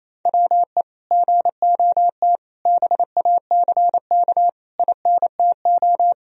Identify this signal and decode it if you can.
Morse code, heard as a keyed tone, about 28 wpm, 700 Hz. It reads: WE GOT BACK INTO
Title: Sentences from Top 1000 Words - Review Part 3 - 28wpm